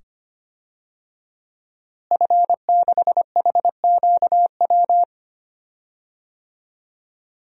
Morse code, heard as a keyed tone, about 25 wpm, 700 Hz. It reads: F6HQW
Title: DXCC-100 Call Signs - 5 Characters - Repeated 2x - 25wpm